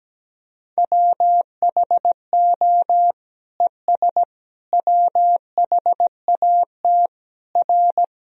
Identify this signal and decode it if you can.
Morse code, heard as a keyed tone, about 17 wpm, 700 Hz. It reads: WHO ES WHAT R
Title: Sentences from Top 100 Words and Abbreviations 17wpm